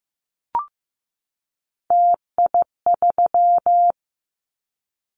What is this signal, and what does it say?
Morse code, heard as a keyed tone, about 15 wpm, 700 Hz. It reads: TI3
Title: Callsign Prefixes for Top 100 DXCC - 15wpm